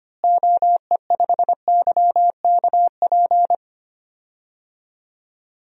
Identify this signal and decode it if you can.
Morse code, heard as a keyed tone, about 25 wpm, 700 Hz. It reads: OE5YKP
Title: DXCC-100 Call Signs - 6 Characters - Repeated 2x - 25wpm